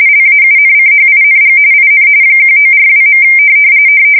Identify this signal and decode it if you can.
RTTY, about 45.45 baud. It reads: LAUNCHING IN 1942.
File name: fsk170 / USS-New-Jersey